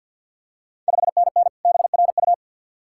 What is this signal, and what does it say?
Morse code, heard as a keyed tone, about 53 wpm, 690 Hz. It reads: 5NN BRU
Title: Ontario QSO Party Exchange 50wpm